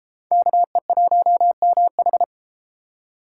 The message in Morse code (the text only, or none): KE1MH